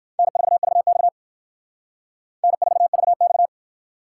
Morse code, heard as a keyed tone, about 59 wpm, 700 Hz. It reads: N4VX N4VX